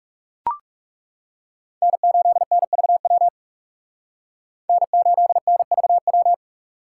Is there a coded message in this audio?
Yes: Morse code from 1.8 to 6.3 s, heard as a keyed tone, about 38 wpm, 700 Hz. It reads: N8NVW N8NVW